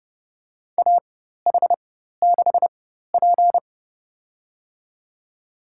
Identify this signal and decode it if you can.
Morse code, heard as a keyed tone, about 32 wpm, spaced as for 14 wpm, 700 Hz. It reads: AH6P
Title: USA Call Signs - 4 Characters - 4x Character Spacing - 30wpm